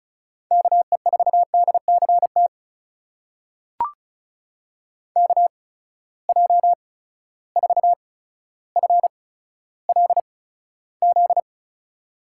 Morse code, heard as a keyed tone, about 36 wpm, spaced as for 15 wpm, 700 Hz. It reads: KE4DCT KJ4FLZ
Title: USA Call Signs - 6 Characters - 8x Character Spacing - 35wpm